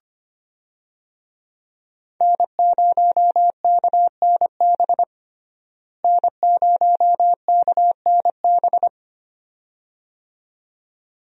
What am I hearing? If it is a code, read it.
Morse code, heard as a keyed tone, about 25 wpm, 700 Hz. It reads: N0KNB N0KNB
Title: USA Call Signs - 5 Characters - Repeated 2x - 25wpm